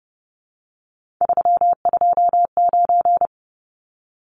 Morse code, heard as a keyed tone, about 30 wpm, 700 Hz. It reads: 329
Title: Numbers - 3-Digit 30wpm